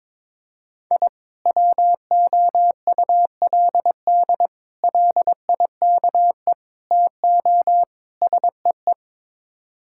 Morse code, heard as a keyed tone, about 21 wpm, 700 Hz. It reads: I WOULD LIKE TO SEE
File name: practice / 22wpm / Build A Phrase - 5-Word Phrases - 22wpm